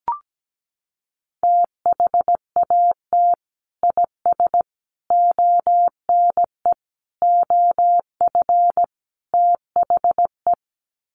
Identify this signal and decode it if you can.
Morse code, heard as a keyed tone, about 17 wpm, 700 Hz. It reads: THAT IS ONE OF THE
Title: Sets of 5 Words Encore 17wpm